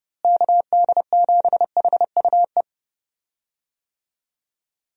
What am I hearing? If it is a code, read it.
Morse code, heard as a keyed tone, about 30 wpm, 700 Hz. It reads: KD7HUE
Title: USA Call Signs - 6 Characters - Repeated 3x - 30wpm